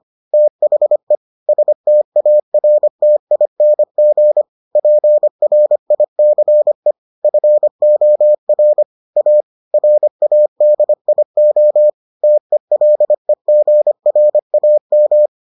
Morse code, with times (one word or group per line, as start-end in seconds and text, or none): 0.3-1.2 THE
1.5-4.4 STARTING
4.8-6.9 PRICE
7.2-8.8 FOR
9.2-9.4 A
9.7-11.9 RADIO
12.2-15.3 TELEGRAM